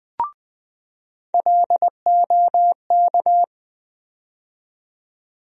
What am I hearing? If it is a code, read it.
Morse code, heard as a keyed tone, about 20 wpm, 700 Hz. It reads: LOK
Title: USA Call Sign Postfixes - 3 Letters 20wpm